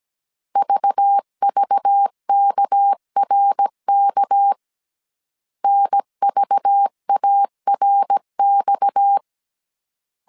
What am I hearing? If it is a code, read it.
Morse code, heard as a keyed tone, about 17 wpm, 800 Hz. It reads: VVKRK NVARX
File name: 16wpm